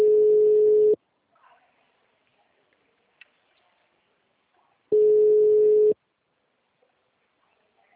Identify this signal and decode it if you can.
Ringback tone (Europe, 425 Hz).